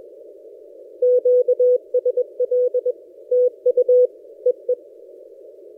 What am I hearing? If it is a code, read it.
Morse code, heard as a keyed tone, about 21 wpm, 500 Hz. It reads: QSL TU EE
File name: posts / a35jr-12cw- -2336z